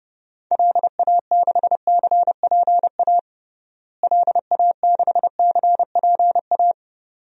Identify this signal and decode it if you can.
Morse code, heard as a keyed tone, about 30 wpm, 700 Hz. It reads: LA6CPA LA6CPA
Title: DXCC-100 Call Signs - 6 Characters - Repeated 2x - 30wpm